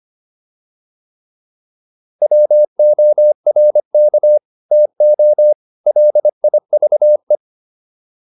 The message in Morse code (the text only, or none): WORK TO LIVE